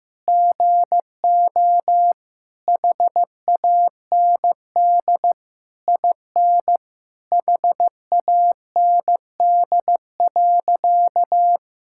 Morse code, heard as a keyed tone, about 15 wpm, 700 Hz. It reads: GO HAND IN HAND.
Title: Sentences from Top 1000 Words - Review Part 5 - 15wpm